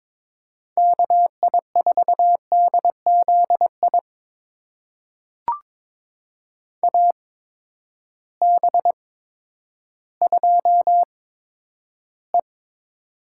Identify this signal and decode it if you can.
Morse code, heard as a keyed tone, about 22 wpm, spaced as for 11 wpm, 700 Hz. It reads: KI4DZI AB2E
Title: USA Call Signs - 8x Character Spacing - 22wpm